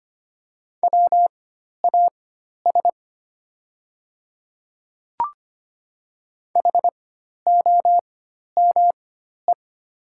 Morse code, heard as a keyed tone, about 25 wpm, spaced as for 12 wpm, 700 Hz. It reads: WAS HOME
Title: 4x Character Spacing - Top 300 Words - 25wpm